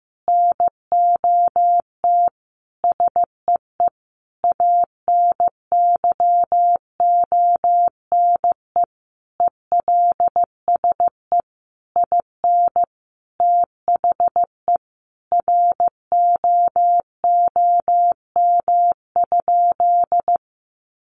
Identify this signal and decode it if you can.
Morse code, heard as a keyed tone, about 15 wpm, 700 Hz. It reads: NOT SEE ANYONE ELSE IN THE ROOM?
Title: Sentences from Top 900 Words 15wpm